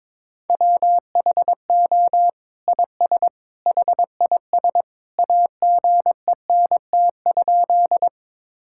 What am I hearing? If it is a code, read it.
Morse code, heard as a keyed tone, about 22 wpm, 700 Hz. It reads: WHO IS HIS AGENT?
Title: Sentences from Top 1600 Words 22wpm